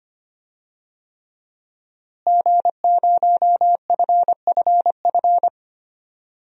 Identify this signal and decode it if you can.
Morse code, heard as a keyed tone, about 25 wpm, 700 Hz. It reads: G0FFF
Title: DXCC-100 Call Signs - 5 Characters - Repeated 3x - 25wpm